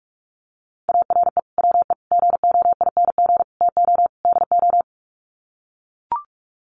Morse code, heard as a keyed tone, about 44 wpm, 700 Hz. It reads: ARE WE GOING TO DO
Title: Sets of 5 Words 45wpm